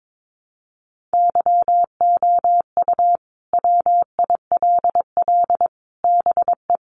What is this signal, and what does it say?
Morse code, heard as a keyed tone, about 22 wpm, 700 Hz. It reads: YOU WILL BE
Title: Sets of 5 Words Encore2 22wpm